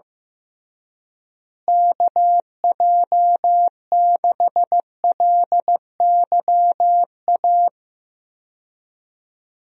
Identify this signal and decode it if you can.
Morse code, heard as a keyed tone, about 16 wpm, 700 Hz. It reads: KJ6LYA
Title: USA Call Signs - 6 Characters - Repeated 2x - 15wpm